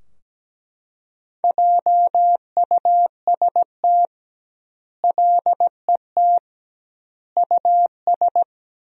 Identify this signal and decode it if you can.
Morse code, heard as a keyed tone, about 17 wpm, 700 Hz. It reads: JUST LET US
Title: 2x Word-Spacing - Sentences from Top 500 Words - 17wpm